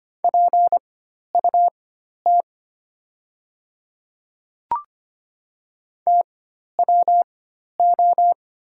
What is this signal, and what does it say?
Morse code, heard as a keyed tone, about 25 wpm, spaced as for 12 wpm, 700 Hz. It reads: PUT TWO
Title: 4x Character Spacing - Top 200 Words - 25wpm